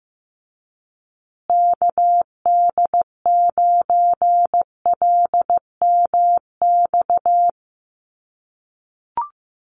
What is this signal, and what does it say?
Morse code, heard as a keyed tone, about 15 wpm, 700 Hz. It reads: KD9LMX
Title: USA Call Signs Repeated 3x 15wpm